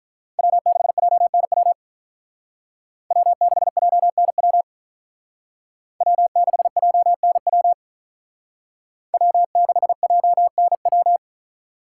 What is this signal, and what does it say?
Morse code, heard as a keyed tone, about 45 wpm, 700 Hz. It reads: W6JNW W6JNW W6JNW W6JNW